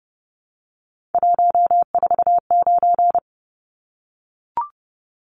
Morse code, heard as a keyed tone, about 30 wpm, 700 Hz. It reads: 149
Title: Numbers - 3-Digit 30wpm